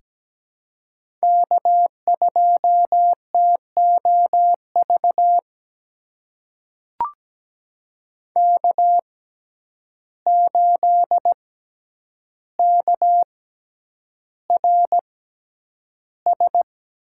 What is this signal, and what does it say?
Morse code, heard as a keyed tone, about 17 wpm, spaced as for 9 wpm, 700 Hz. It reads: K2TOV K8KRS
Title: USA Call Signs - 5 Characters - 6x Character Spacing - 17wpm